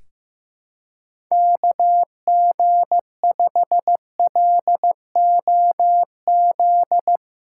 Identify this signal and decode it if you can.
Morse code, heard as a keyed tone, about 16 wpm, 700 Hz. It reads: KG5LOZ
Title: USA Callsign DE Callsign - 6 Characters - 15wpm